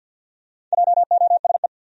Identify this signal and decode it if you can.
Morse code, heard as a keyed tone, about 52 wpm, 700 Hz. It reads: JOSE
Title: USA Common Names - Max 5 Letters - 50wpm